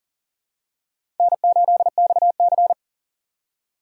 Morse code, heard as a keyed tone, about 40 wpm, 700 Hz. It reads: N8XC
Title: USA Call Signs - 4 Characters - 6x Character Spacing - 40wpm